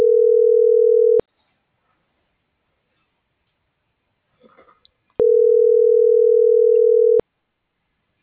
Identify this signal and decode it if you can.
Ringback tone (North America), 440 + 480 Hz.
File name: CENTRAL MULTIMODAL / out cecisandoval